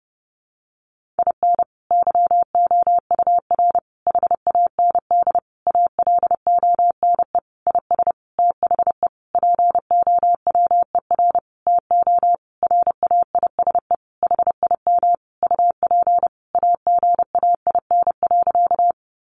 Morse code, heard as a keyed tone, about 30 wpm, 700 Hz. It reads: IN YOUR HAND ALONE IS THE POWER TO RAISE HIM UP AGAIN.
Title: Sentences from Top 1400 Words 30wpm